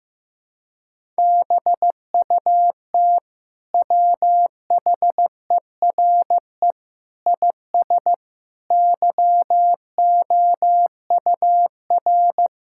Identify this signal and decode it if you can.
Morse code, heard as a keyed tone, about 15 wpm, 700 Hz. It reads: BUT WHERE IS YOUR
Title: Sentences from Top 1000 Words - Review Part 5 - 15wpm